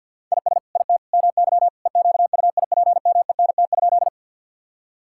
Morse code, heard as a keyed tone, about 50 wpm, 700 Hz. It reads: IS IT MY EQUIPMENT?
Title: Sentences from Top 1000 Words - Review Part 5 - 50wpm